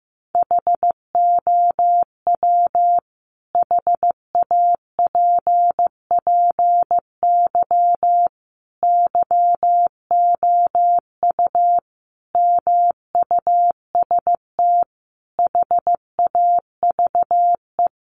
Morse code, heard as a keyed tone, about 14 wpm, 700 Hz. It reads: HOW HAPPY YOU MUST HAVE
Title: Sentences from Top 1000 Words 15wpm